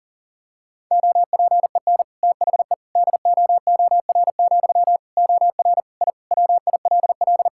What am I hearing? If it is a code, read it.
Morse code, heard as a keyed tone, about 37 wpm, 700 Hz. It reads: OPEN THE DOOR, OR I WILL
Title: Sentences from Top 1000 Words - Review Part 6 - 40wpm